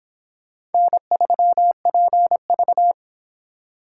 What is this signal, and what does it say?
Morse code, heard as a keyed tone, about 29 wpm, 700 Hz. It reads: N3PV